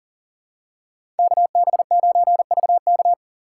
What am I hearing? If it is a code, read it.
Morse code, heard as a keyed tone, about 39 wpm, 700 Hz. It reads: KB9VK
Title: USA Call Signs - 5 Characters - 2x Character Spacing - 40wpm